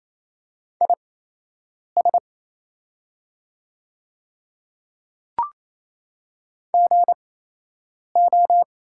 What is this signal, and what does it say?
Morse code, heard as a keyed tone, about 28 wpm, spaced as for 8 wpm, 700 Hz. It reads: IS GO
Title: 8x Character Spacing - Top 100 Words - 28wpm